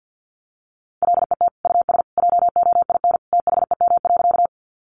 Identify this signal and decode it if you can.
Morse code, heard as a keyed tone, about 50 wpm, 700 Hz. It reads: LET US JOIN THEM.
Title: Sentences from Top 1400 Words 50wpm